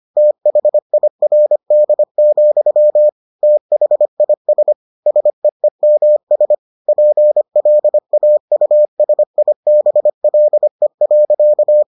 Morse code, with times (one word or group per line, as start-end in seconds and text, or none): 0.2-3.1 THIRD,
3.4-4.7 THIS
5.1-6.5 SEEMS
6.9-11.8 PLAUSIBLE.